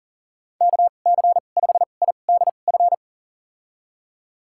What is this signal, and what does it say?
Morse code, heard as a keyed tone, about 38 wpm, spaced as for 28 wpm, 700 Hz. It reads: KC5IDF
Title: USA Call Signs - 6 Characters - 2x Character Spacing - 40wpm